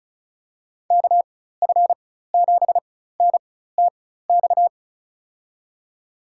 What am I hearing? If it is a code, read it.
Morse code, heard as a keyed tone, about 36 wpm, spaced as for 16 wpm, 700 Hz. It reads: KF7NTX
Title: USA Call Signs - 6 Characters - 4x Character Spacing - 35wpm